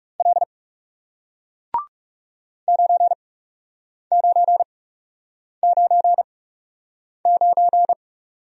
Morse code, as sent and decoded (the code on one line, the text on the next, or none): .-.. / ----. ----. ----. ----.
L 9999